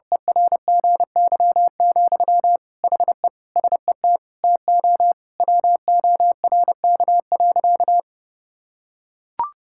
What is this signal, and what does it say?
Morse code, heard as a keyed tone, about 30 wpm, 700 Hz. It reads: ERGY, HE SET TO WORK.
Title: Sentences from Top 1000 Words - Review Part 6 - 30wpm